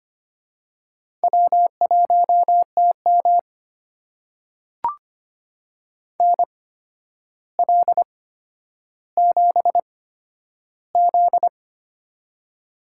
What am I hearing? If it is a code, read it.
Morse code, heard as a keyed tone, about 25 wpm, spaced as for 11 wpm, 700 Hz. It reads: W1TM NL7Z
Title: USA Call Signs - 4 Characters - 8x Character Spacing - 25wpm